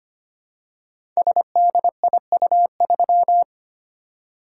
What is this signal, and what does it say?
Morse code, heard as a keyed tone, about 25 wpm, 700 Hz. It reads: SDIU3
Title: Code Groups - 5 Letter:Numbers - 25wpm